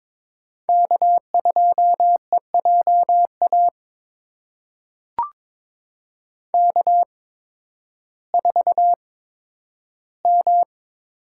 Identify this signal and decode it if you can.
Morse code, heard as a keyed tone, about 22 wpm, spaced as for 12 wpm, 700 Hz. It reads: K2EJA K4M
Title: USA Call Signs - 5 Characters - 8x Character Spacing - 22wpm